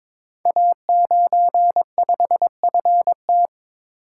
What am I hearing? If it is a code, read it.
Morse code, heard as a keyed tone, about 22 wpm, 700 Hz. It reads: A95FT
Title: Code Groups - 5 Letter:Numbers - 22wpm